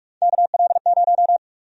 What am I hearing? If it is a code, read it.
Morse code, heard as a keyed tone, about 46 wpm, 700 Hz. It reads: KL0
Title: USA Call Sign Prefixes - Repeated 2x - 45wpm